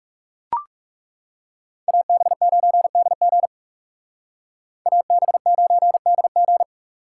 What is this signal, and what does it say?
Morse code, heard as a keyed tone, about 36 wpm, 700 Hz. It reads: AB9DG AB9DG
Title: Speed Racing - USA 2x2 Call Signs 30wpm